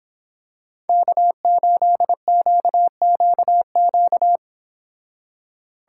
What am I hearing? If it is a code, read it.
Morse code, heard as a keyed tone, about 26 wpm, 700 Hz. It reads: K8QQQ